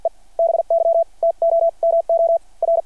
Morse code, heard as a keyed tone, about 53 wpm, spaced as for 38 wpm, 660 Hz. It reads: E BY TOMOR
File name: archived / CRSnet